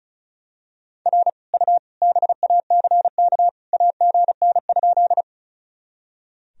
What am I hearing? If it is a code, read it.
Morse code, heard as a keyed tone, about 36 wpm, 700 Hz. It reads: R U BACK AGN?
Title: Sentences from Top 200 Words and Abbreviations 35wpm